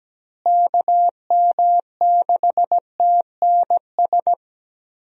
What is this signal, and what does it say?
Morse code, heard as a keyed tone, about 17 wpm, 700 Hz. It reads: KM6TNS
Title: USA Call Signs - 6 Characters - Repeated 2x - 17wpm